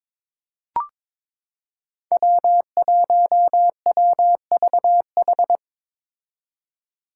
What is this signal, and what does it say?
Morse code, heard as a keyed tone, about 22 wpm, 700 Hz. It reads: W1WVH
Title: Five Challenging Letter-Number Combinations 22wpm